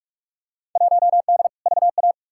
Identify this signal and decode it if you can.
Morse code, heard as a keyed tone, about 46 wpm, 700 Hz. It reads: 1D VA
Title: Field Day Exchanges - 45wpm